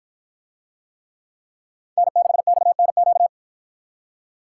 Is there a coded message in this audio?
Yes: Morse code from 2.0 to 3.3 s, heard as a keyed tone, about 53 wpm, 700 Hz. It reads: N6XNQ